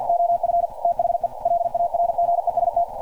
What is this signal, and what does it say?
Morse code, heard as a keyed tone, about 48 wpm, 690 Hz. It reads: OF THE LARGEST
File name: archived / CRSnet